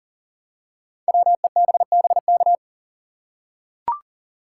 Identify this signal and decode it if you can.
Morse code, heard as a keyed tone, about 42 wpm, 690 Hz. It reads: WEBBK
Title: Code Groups - 5 Letter:Numbers - 40wpm